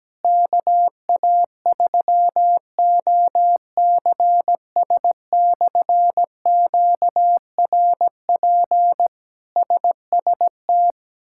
Morse code, heard as a keyed tone, about 17 wpm, 700 Hz. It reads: KA3OCS/QRP SST
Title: K1USN SST QSO Practice - 17wpm